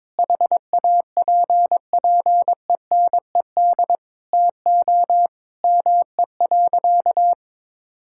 Morse code, heard as a keyed tone, about 22 wpm, 700 Hz. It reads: HAPPENED TO ME.
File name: practice / 22wpm / Sentences with 8-Word Phrases - 22wpm